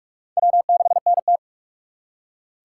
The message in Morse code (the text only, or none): W6NT